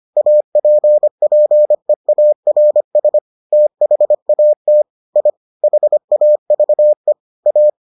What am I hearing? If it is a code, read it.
Morse code, heard as a keyed tone, about 25 wpm, 600 Hz. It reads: APPEARS THAT I HAVE A